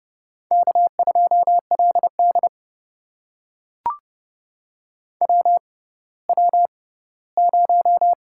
Morse code, heard as a keyed tone, about 31 wpm, spaced as for 17 wpm, 700 Hz. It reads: K2LD WW0
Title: USA Call Signs - 4 Characters - 6x Character Spacing - 30wpm